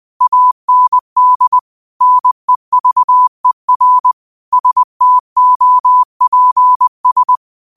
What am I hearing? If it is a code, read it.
Morse code, heard as a keyed tone, about 20 wpm, 1000 Hz. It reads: AND NEVER STOPS
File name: 20wpm